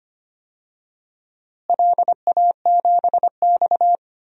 Morse code, heard as a keyed tone, about 25 wpm, 700 Hz. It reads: LA7X
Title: DXCC-100 Call Signs - 4 Characters - Repeated 3x - 25wpm